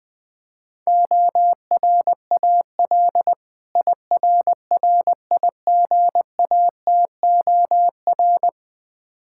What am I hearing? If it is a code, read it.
Morse code, heard as a keyed tone, about 20 wpm, 700 Hz. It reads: ORAL IRRIGATOR
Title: Difficult Words - Speed-Racing with Hints - 20wpm